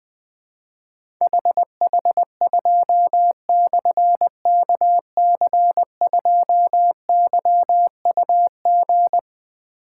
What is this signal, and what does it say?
Morse code, heard as a keyed tone, about 20 wpm, 700 Hz. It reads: HH2/KC2YUG
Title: Difficult Call Signs 20wpm